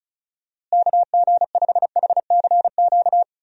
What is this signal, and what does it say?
Morse code, heard as a keyed tone, about 36 wpm, 700 Hz. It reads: KG5HCQ
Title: USA Call Signs - 6 Characters - Repeated 2x - 35wpm